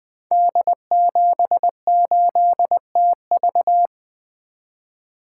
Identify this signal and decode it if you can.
Morse code, heard as a keyed tone, about 20 wpm, 700 Hz. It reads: D78TV
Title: Code Groups - 5 Letter:Numbers - 20wpm